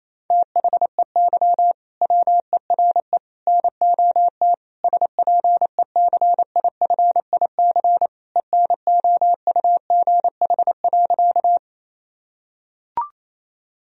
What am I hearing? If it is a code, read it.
Morse code, heard as a keyed tone, about 27 wpm, 700 Hz. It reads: THEY WERE NOT SPECIFIC ENOUGH.
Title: Sentences from Top 1000 Words - Review Part 3 - 28wpm